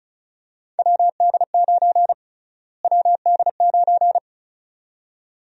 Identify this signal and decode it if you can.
Morse code, heard as a keyed tone, about 34 wpm, 700 Hz. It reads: WD9 WD9
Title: USA Call Sign Prefixes - Repeated 2x - 35wpm